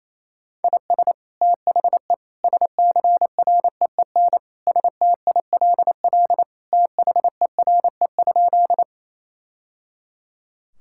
Morse code, heard as a keyed tone, about 28 wpm, 700 Hz. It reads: IS THE SCREEN STILL THERE?
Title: Sentences from Top 1000 Words - Review Part 6 - 28wpm